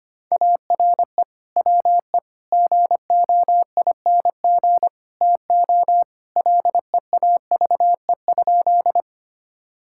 Morse code, heard as a keyed tone, about 25 wpm, 700 Hz. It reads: ARE WE GOING TO LEAVE?
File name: practice / 25wpm / Sentences with 4-Word Phrases - 25wpm